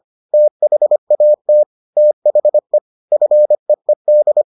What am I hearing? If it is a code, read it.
Morse code, heard as a keyed tone, about 25 wpm, 600 Hz. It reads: THAT THE FEED